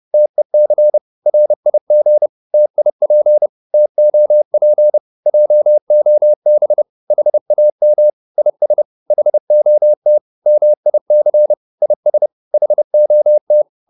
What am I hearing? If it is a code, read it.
Morse code, heard as a keyed tone, about 30 wpm, 600 Hz. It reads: TEC RIG TIP TOP JOB HAM IS HOT MIC IS HOT